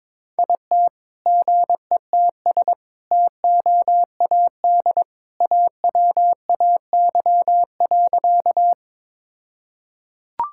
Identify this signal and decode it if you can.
Morse code, heard as a keyed tone, about 21 wpm, 700 Hz. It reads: IT GETS TOAD AWAY.
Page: morse-code-ninja-practice-22wpm